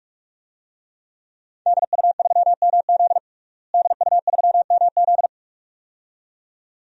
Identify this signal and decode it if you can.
Morse code, heard as a keyed tone, about 46 wpm, 700 Hz. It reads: DU3MZ DU3MZ
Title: DXCC-100 Call Signs - 5 Characters - Repeated 2x - 45wpm